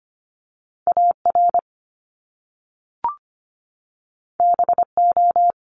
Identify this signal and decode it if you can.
Morse code, heard as a keyed tone, about 25 wpm, 700 Hz. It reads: AR BO
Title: Top Two Letter Combinations in Words 25wpm